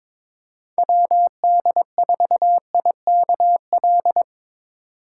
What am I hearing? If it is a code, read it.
Morse code, heard as a keyed tone, about 24 wpm, 700 Hz. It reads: WD4IKL
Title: Speed Racing - USA 2x3 Call Signs 22wpm